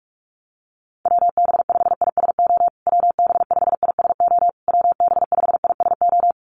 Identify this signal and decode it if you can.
Morse code, heard as a keyed tone, about 44 wpm, 690 Hz. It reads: WB5ISO WB5ISO WB5ISO
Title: USA Call Signs Repeated 3x 45wpm